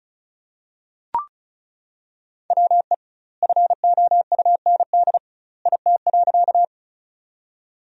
Morse code, heard as a keyed tone, about 34 wpm, 700 Hz. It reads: WE FOUND IT.
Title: 2x Word-Spacing - Sentences from Top 300 Words - 35wpm